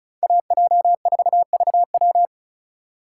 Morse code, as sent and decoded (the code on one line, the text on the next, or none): .- .--- ....- ...- .--
AJ4VW